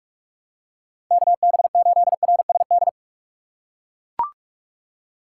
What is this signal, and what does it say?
Morse code, heard as a keyed tone, about 41 wpm, 700 Hz. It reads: KB8RSD
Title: USA Call Signs - 6 Characters - 3x Character Spacing - 45wpm